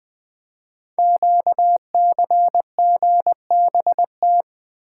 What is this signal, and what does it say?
Morse code, heard as a keyed tone, about 20 wpm, 700 Hz. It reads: QCGBT
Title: Code Groups - 5 Letter:Numbers - 20wpm